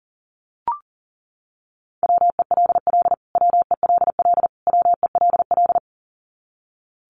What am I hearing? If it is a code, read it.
Morse code, heard as a keyed tone, about 40 wpm, 700 Hz. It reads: WELL WELL WELL
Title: Top 500 Words Repeated 40wpm